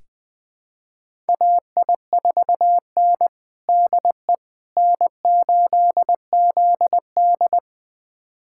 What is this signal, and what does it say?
Morse code, heard as a keyed tone, about 21 wpm, 700 Hz. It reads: AI4N DE N8ZD
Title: USA Callsign DE Callsign - 4 Characters - 20wpm